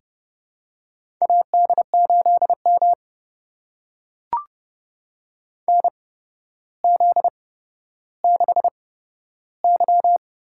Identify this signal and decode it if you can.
Morse code, heard as a keyed tone, about 30 wpm, spaced as for 13 wpm, 700 Hz. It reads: AD8M NZ6Y
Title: USA Call Signs - 4 Characters - 8x Character Spacing - 30wpm